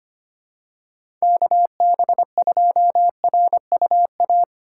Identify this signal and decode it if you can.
Morse code, heard as a keyed tone, about 25 wpm, 700 Hz. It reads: KB2RUA
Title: USA Call Signs - 6 Characters - 4x Character Spacing - 25wpm